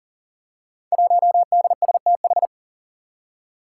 Morse code, heard as a keyed tone, about 40 wpm, 700 Hz. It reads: 1DSTH
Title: Five Challenging Letter-Number Combinations 40wpm